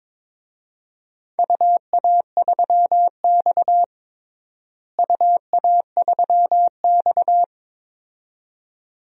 Morse code, heard as a keyed tone, about 22 wpm, 700 Hz. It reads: UA3X UA3X
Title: DXCC-100 Call Signs - 4 Characters - Repeated 2x - 22wpm